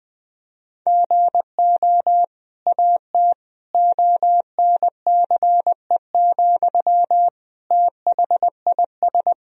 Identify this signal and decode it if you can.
Morse code, heard as a keyed tone, about 20 wpm, 700 Hz. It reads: GO AT ONCE, THIS
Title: Sentences from Top 1000 Words - Review Part 1 - 20wpm